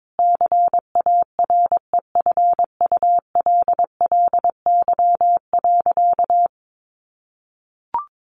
Morse code, heard as a keyed tone, about 22 wpm, 700 Hz. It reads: CAREFULLY.
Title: Sentences from Top 1300 Words 22wpm